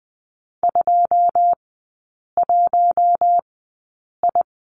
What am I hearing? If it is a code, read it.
Morse code, heard as a keyed tone, about 20 wpm, spaced as for 8 wpm, 700 Hz. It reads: 21I
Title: 087 Lesson 15e - Number 5 - Three Characters 20wpm